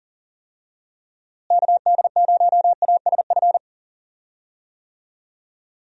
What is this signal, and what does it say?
Morse code, heard as a keyed tone, about 37 wpm, 700 Hz. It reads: KD0ASF
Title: USA Call Signs - 6 Characters - Repeated 3x - 40wpm